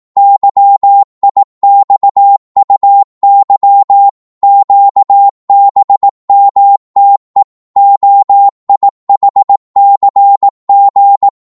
Morse code, heard as a keyed tone, about 18 wpm, 800 Hz. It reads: YIXUY QBMTE OIHCG
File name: Morse / ltr-18wpm